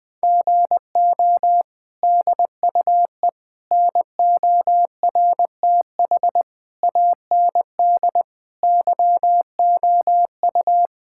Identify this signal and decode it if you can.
Morse code, heard as a keyed tone, about 20 wpm, 700 Hz. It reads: GO DUE NORTH AND YOU
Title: Sentences from Top 1000 Words - Review Part 6 - 20wpm